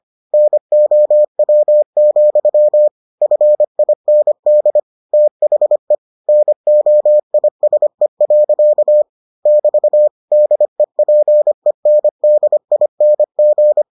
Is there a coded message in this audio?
Yes: Morse code from 0.3 to 13.8 s, heard as a keyed tone, about 25 wpm, 600 Hz. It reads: NOW, FIND THE NOISE. <BT> DEPENDING